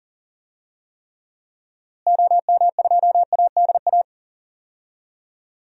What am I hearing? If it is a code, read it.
Morse code, heard as a keyed tone, about 37 wpm, 700 Hz. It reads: OM2ADA
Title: DXCC-100 Call Signs - 6 Characters - Repeated 3x - 40wpm